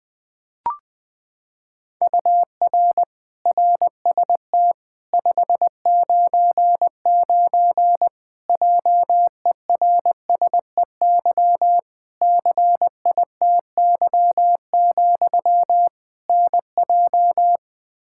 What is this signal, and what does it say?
Morse code, heard as a keyed tone, about 20 wpm, 700 Hz. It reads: UR RST 599 JERSEY CITY, NJ
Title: Practice QSO - Exchange 1 - Beginning - 20wpm